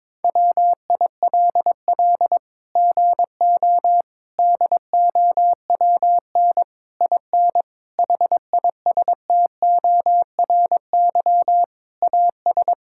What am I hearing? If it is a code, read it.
Morse code, heard as a keyed tone, about 22 wpm, 700 Hz. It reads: WILL GO DOWN IN HISTORY AS
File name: Sets of 9 Words - 22wpm